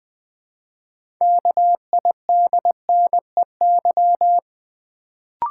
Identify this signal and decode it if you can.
Morse code, heard as a keyed tone, about 20 wpm, 700 Hz. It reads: KIDNEY
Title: Build A Word - 2 Letters At A Time - 6-Letter Words - 20wpm